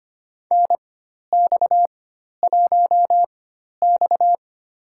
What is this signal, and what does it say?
Morse code, heard as a keyed tone, about 25 wpm, spaced as for 12 wpm, 700 Hz. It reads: NX1X
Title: USA Call Signs - 4x Character Spacing - 25wpm